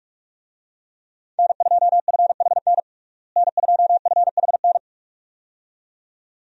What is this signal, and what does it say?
Morse code, heard as a keyed tone, about 46 wpm, 700 Hz. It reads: N2FHN N2FHN
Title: USA Call Signs - 5 Characters - Repeated 2x - 45wpm